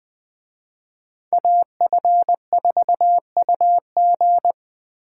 Morse code, heard as a keyed tone, about 20 wpm, 700 Hz. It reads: AF4UG